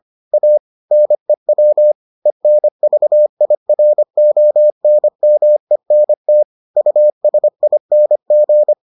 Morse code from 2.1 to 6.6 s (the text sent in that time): ENVIRONMENT